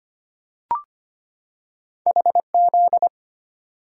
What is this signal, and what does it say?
Morse code, heard as a keyed tone, about 25 wpm, 700 Hz. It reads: HZ
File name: Two Letter-Number Combinations - 25wpm